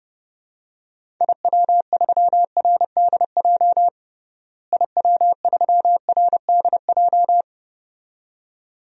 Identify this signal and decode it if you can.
Morse code, heard as a keyed tone, about 32 wpm, 700 Hz. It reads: IW3RDJ IW3RDJ
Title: DXCC-100 Call Signs - 6 Characters - Repeated 2x - 30wpm